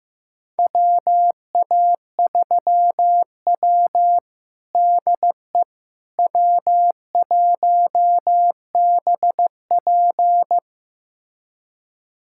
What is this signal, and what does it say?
Morse code, heard as a keyed tone, about 15 wpm, 700 Hz. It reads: WA3W DE W1BP
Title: USA Callsign DE Callsign - 4 Characters - 15wpm